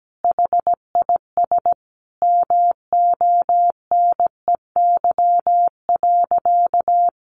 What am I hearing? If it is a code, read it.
Morse code, heard as a keyed tone, about 17 wpm, 700 Hz. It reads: HIS MONEY.
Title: Sentences from Top 1200 Words 17wpm